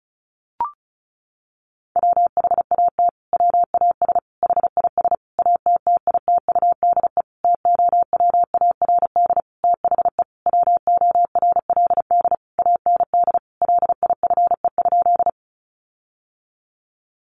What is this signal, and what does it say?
Morse code, heard as a keyed tone, about 34 wpm, 700 Hz. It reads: WHAT WAS HIS ATTITUDE TOWARD THE WORLD AND LIFE?
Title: Sentences from Top 1300 Words 35wpm